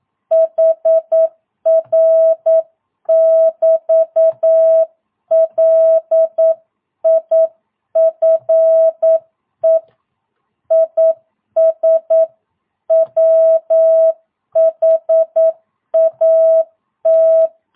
Morse code, heard as a keyed tone, about 9 wpm, 660 Hz. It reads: HR <BT> LIFE IS WHAT
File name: jh8cbh / 2022cw